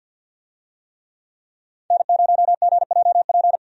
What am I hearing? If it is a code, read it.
Morse code, heard as a keyed tone, about 50 wpm, 700 Hz. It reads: N0GJP
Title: USA Call Signs - 5 Characters - 50wpm